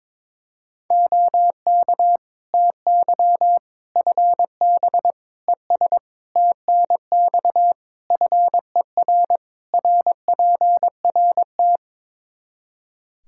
Morse code, heard as a keyed tone, about 22 wpm, 700 Hz. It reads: OK TY FB ES TNX FER RPRT
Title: Practice QSO - Exchange 2 - Info - 22wpm